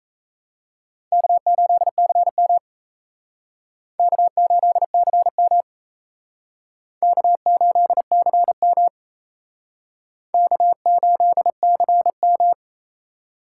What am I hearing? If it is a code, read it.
Morse code, heard as a keyed tone, about 36 wpm, 700 Hz. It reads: K8CM K8CM K8CM K8CM